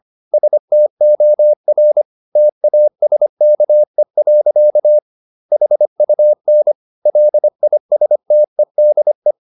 Morse code, heard as a keyed tone, about 25 wpm, 600 Hz. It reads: STOR TASKE. HUN LISTEDE
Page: Ferie 25 wpm